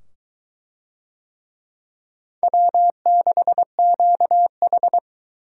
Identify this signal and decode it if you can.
Morse code, heard as a keyed tone, about 22 wpm, 700 Hz. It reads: W6QH